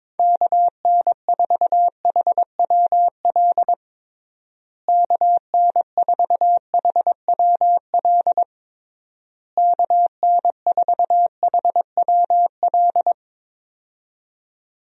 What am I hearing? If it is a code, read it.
Morse code, heard as a keyed tone, about 22 wpm, 700 Hz. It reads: KN4HWL KN4HWL KN4HWL
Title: USA Call Signs - 6 Characters - Repeated 3x - 22wpm